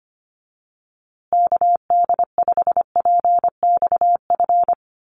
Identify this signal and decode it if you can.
Morse code, heard as a keyed tone, about 25 wpm, 700 Hz. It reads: KD5PXF
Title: USA Call Signs Repeated 3x 25wpm